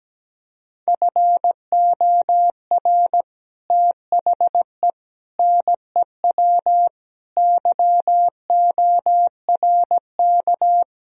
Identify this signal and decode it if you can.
Morse code, heard as a keyed tone, about 17 wpm, 700 Hz. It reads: FOR THE NEW YORK
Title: Speed-Racing - Sets of 5 Words 17wpm